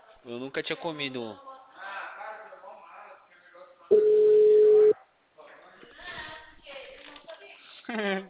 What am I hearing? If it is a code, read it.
Ringback tone (Europe, 425 Hz), with speech.
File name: out- -2003- - - .53566